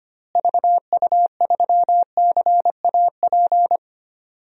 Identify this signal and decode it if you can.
Morse code, heard as a keyed tone, about 25 wpm, 700 Hz. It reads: VU3CAP
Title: DXCC-100 Call Signs - 6 Characters - Repeated 3x - 25wpm